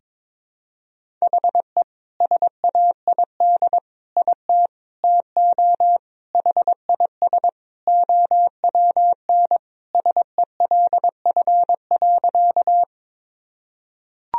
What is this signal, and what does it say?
Morse code, heard as a keyed tone, about 21 wpm, 700 Hz. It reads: HE SAID IT TO HIS OWN SELF.
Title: Sentences from Top 1000 Words - Review Part 1 - 22wpm